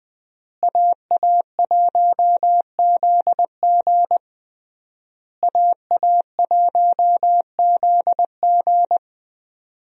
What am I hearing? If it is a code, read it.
Morse code, heard as a keyed tone, about 21 wpm, 700 Hz. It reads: AA1ZG AA1ZG
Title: USA Call Signs - 5 Characters - Repeated 3x - 20wpm